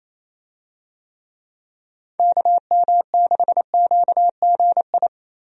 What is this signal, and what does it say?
Morse code, heard as a keyed tone, about 27 wpm, 700 Hz. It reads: KM6QGI
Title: USA Call Signs - 6 Characters - Repeated 2x - 28wpm